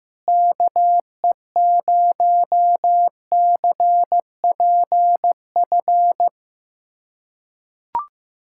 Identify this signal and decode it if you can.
Morse code, heard as a keyed tone, about 15 wpm, 700 Hz. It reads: KE0CPF
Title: USA Callsign DE Callsign - 6 Characters - 15wpm